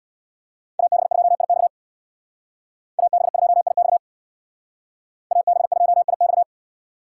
Morse code, heard as a keyed tone, about 63 wpm, 700 Hz. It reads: AB2IX AB2IX AB2IX